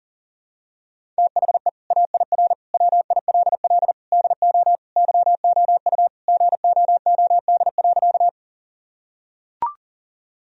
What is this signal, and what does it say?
Morse code, heard as a keyed tone, about 40 wpm, 700 Hz. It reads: THE AIR WILL DO YOU GOOD.
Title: Sentences from Top 1000 Words - Review Part 1 - 40wpm